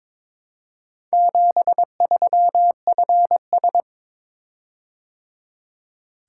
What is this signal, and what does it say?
Morse code, heard as a keyed tone, about 22 wpm, 700 Hz. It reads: 73FS
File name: practice / 22wpm / Four Letter-Number Combinations - 22wpm